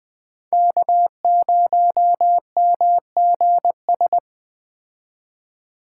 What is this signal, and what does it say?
Morse code, heard as a keyed tone, about 20 wpm, 700 Hz. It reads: K0MGS
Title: Speed Racing - USA 1x3 Call Signs 17wpm